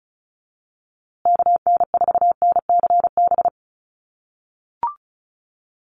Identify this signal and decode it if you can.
Morse code, heard as a keyed tone, about 34 wpm, 700 Hz. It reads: KN4NCB
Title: USA Call Signs Repeated 3x 35wpm